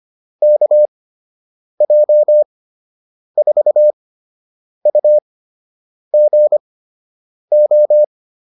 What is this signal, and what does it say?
Morse code, heard as a keyed tone, about 25 wpm, spaced as for 8 wpm, 600 Hz. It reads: KJ4UGO